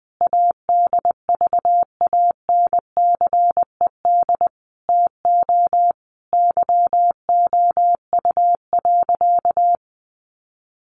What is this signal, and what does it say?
Morse code, heard as a keyed tone, about 20 wpm, 700 Hz. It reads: ADVANCED TO YOU.
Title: Sentences from Top 1200 Words 20wpm